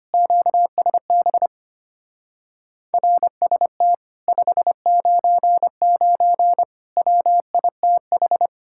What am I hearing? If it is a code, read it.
Morse code, heard as a keyed tone, about 33 wpm, 700 Hz. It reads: QSB RST 599 WITH